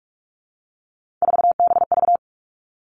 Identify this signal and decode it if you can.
Morse code, heard as a keyed tone, about 46 wpm, 700 Hz. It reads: 4BV
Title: Three Challenging Letter-Number Combinations 45wpm